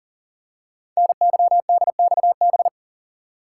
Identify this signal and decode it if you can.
Morse code, heard as a keyed tone, about 42 wpm, 690 Hz. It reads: NYDXB
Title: Code Groups - 5 Letters - 40wpm